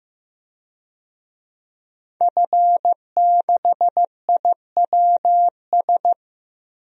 Morse code, heard as a keyed tone, about 15 wpm, 700 Hz. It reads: F6IWS
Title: DXCC-100 Call Signs - 5 Characters - 15wpm